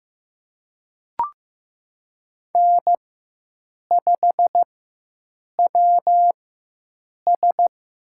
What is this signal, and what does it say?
Morse code, heard as a keyed tone, about 15 wpm, spaced as for 7 wpm, 700 Hz. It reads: N5WS
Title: USA Call Signs - 4 Characters - 4x Character Spacing - 15wpm